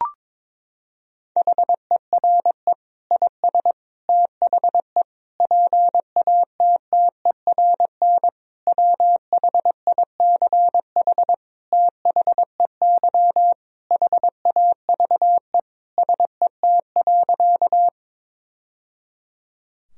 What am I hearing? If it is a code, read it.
Morse code, heard as a keyed tone, about 22 wpm, 700 Hz. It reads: HERE IS THE PATTERN WHICH THEY HAVE SET.
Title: Sentences from Top 1000 Words - Review Part 2 - 22wpm